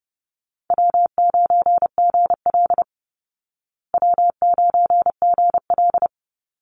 Morse code, heard as a keyed tone, about 30 wpm, 700 Hz. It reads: W9GL W9GL
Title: Call Signs Repeated 2x - 30wpm